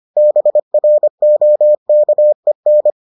Morse code, heard as a keyed tone, about 25 wpm, 600 Hz. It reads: BROKEN